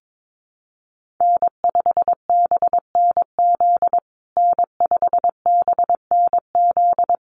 Morse code, heard as a keyed tone, about 22 wpm, 700 Hz. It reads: N5BNZ N5BNZ
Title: USA Call Signs Repeated 3x 22wpm